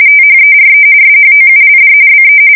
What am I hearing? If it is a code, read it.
RTTY, about 45.45 baud. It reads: THERE SHOULD BE
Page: 170 Shift 2125/2295